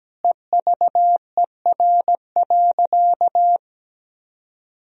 Morse code, heard as a keyed tone, about 17 wpm, 700 Hz. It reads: EVER.
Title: Sentences from Top 1000 Words - Review Part 2 - 17wpm